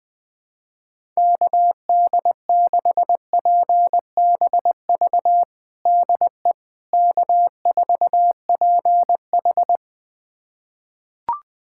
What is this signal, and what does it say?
Morse code, heard as a keyed tone, about 21 wpm, 700 Hz. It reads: KD6PBV DE K4PH
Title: USA Callsign DE Callsign 20wpm